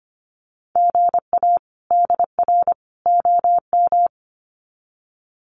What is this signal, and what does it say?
Morse code, heard as a keyed tone, about 25 wpm, 700 Hz. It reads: GA DR OM
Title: Common QSO Phrases 25wpm